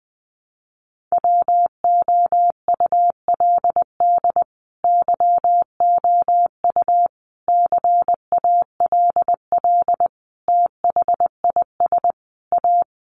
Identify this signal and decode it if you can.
Morse code, heard as a keyed tone, about 20 wpm, 700 Hz. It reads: WOULD YOU CALL THIS A
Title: Sentences from Top 600 Words 20wpm